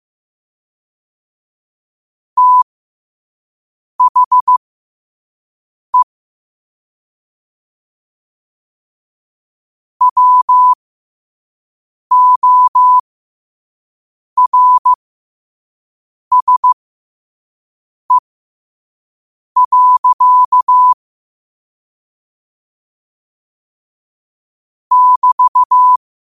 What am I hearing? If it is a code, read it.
Morse code, heard as a keyed tone, about 15 wpm, spaced as for 5 wpm, 1000 Hz. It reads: THE WORSE. <BT>
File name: qotdfiles / 5wpm